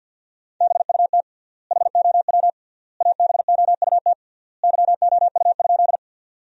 Morse code, heard as a keyed tone, about 50 wpm, 700 Hz. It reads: BUT HOW ABOUT YOU?
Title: 3x Word-Spacing - Sentences from Top 100 Words - 50wpm